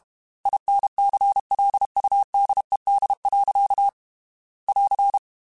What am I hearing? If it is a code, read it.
Morse code, heard as a keyed tone, about 32 wpm, 800 Hz. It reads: INCLUDED. <AR>
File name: Quotes / Quotes-mp3s / Airtight @30